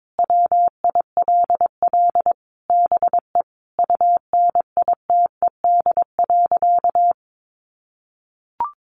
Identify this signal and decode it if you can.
Morse code, heard as a keyed tone, about 22 wpm, 700 Hz. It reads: WILL BE UNITED.
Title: Sentences from Top 600 Words 22wpm